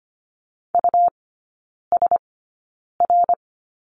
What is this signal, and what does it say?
Morse code, heard as a keyed tone, about 25 wpm, spaced as for 9 wpm, 700 Hz. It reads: USR
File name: course / 25:10wpm / 100 Lesson 17e - Letter u - Three Characters 25wpm